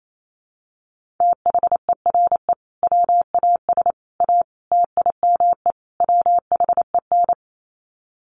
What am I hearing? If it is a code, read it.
Morse code, heard as a keyed tone, about 28 wpm, 700 Hz. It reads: THERE WAS A TIME WHEN
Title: Sets of 5 Words 28wpm